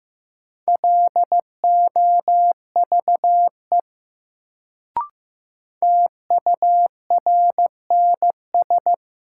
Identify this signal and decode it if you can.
Morse code, heard as a keyed tone, about 15 wpm, 700 Hz. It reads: LOVE TURNS
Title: Sets of 5 Words Encore - New Word Drill - 15wpm